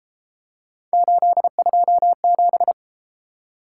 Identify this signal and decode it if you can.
Morse code, heard as a keyed tone, about 31 wpm, 700 Hz. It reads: 827